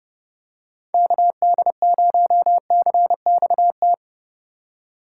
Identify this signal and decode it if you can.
Morse code, heard as a keyed tone, about 29 wpm, 700 Hz. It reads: KD0CXT